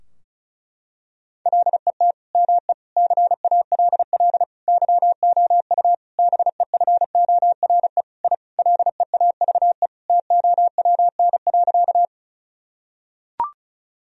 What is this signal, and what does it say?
Morse code, heard as a keyed tone, about 34 wpm, 700 Hz. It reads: LET ME CALL YOU BEFORE I LEAVE TOWN.
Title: Sentences from Top 1000 Words - Review Part 2 - 35wpm